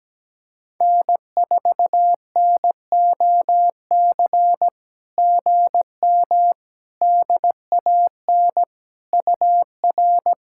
Morse code, heard as a keyed tone, about 17 wpm, 700 Hz. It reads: N4NOC GM DAN UR
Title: SOTA QSO Practice 17wpm